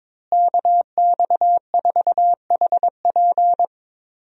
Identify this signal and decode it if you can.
Morse code, heard as a keyed tone, about 22 wpm, 700 Hz. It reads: KX4HP
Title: USA Call Signs - 5 Characters - Repeated 3x - 22wpm